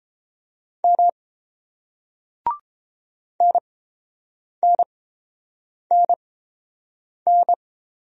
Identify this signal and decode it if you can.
Morse code, heard as a keyed tone, about 30 wpm, spaced as for 7 wpm, 700 Hz. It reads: M NNNN